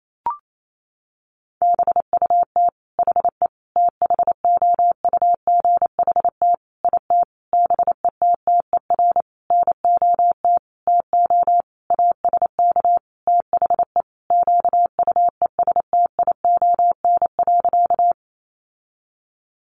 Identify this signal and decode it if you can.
Morse code, heard as a keyed tone, about 28 wpm, 700 Hz. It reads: BUT HE THOUGHT IT BETTER NOT TO ASK THE QUESTION.
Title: Sentences from Top 600 Words 28wpm